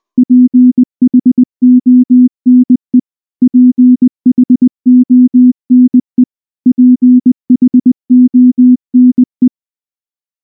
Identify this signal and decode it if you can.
Morse code, heard as a keyed tone, about 19 wpm, 250 Hz. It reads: PHONE PHONE PHONE
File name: morse / morse-training-20WPM-4